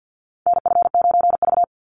Morse code, heard as a keyed tone, about 50 wpm, 700 Hz. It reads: NF9V
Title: Call Signs - 50wpm